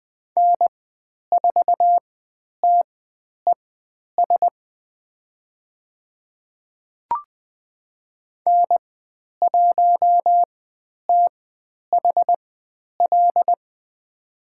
Morse code, heard as a keyed tone, about 20 wpm, spaced as for 10 wpm, 700 Hz. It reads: N4TES N1THL
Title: 102 Lesson 17g - Letter u - Call Signs 20wpm